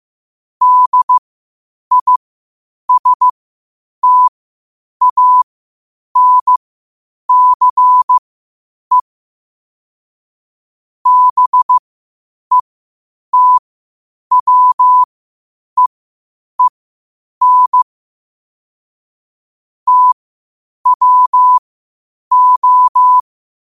Morse code, heard as a keyed tone, about 15 wpm, spaced as for 8 wpm, 1000 Hz. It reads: DISTANCE BETWEEN TWO
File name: qotdfiles / 8wpm